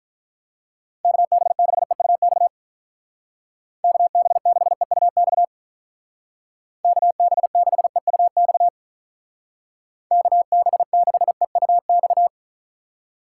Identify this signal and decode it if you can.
Morse code, heard as a keyed tone, about 44 wpm, 700 Hz. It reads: KB6EUX KB6EUX KB6EUX KB6EUX